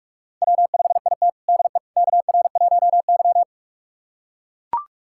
Morse code, heard as a keyed tone, about 43 wpm, 700 Hz. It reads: W5IT DE KR1Y
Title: USA Callsign DE Callsign - 4 Characters - 45wpm